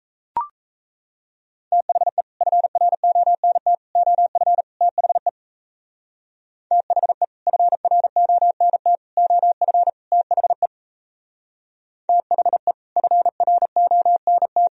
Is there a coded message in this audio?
Yes: Morse code from 1.7 to 14.7 s, heard as a keyed tone, about 36 wpm, 700 Hz. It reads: THE FRONT OF THE THE FRONT OF THE THE FRONT